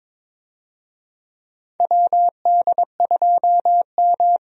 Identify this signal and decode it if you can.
Morse code, heard as a keyed tone, about 22 wpm, 700 Hz. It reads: WD2M
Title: USA Call Signs - 4 Characters - 22wpm